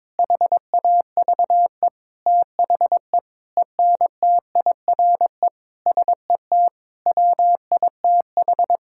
Morse code, heard as a keyed tone, about 22 wpm, 700 Hz. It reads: HAVE THE ENTIRE SET WITH
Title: Sentences from Top 1000 Words - Review Part 5 - 22wpm